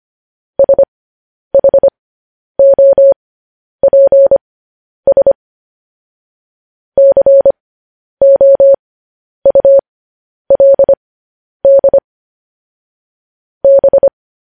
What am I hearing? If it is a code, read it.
Morse code, heard as a keyed tone, about 25 wpm, spaced as for 10 wpm, 550 Hz. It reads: SHOPS COULD B
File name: podcast / News headlines-1x-25x10